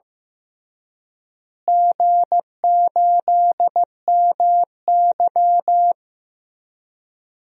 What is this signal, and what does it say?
Morse code, heard as a keyed tone, about 15 wpm, 700 Hz. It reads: G8MY
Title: DXCC-100 Call Signs - 4 Characters - Repeated 3x - 15wpm